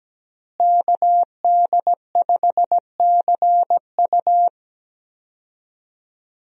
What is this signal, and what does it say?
Morse code, heard as a keyed tone, about 17 wpm, 700 Hz. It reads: KD5CU
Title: USA Call Signs - 5 Characters - Repeated 3x - 17wpm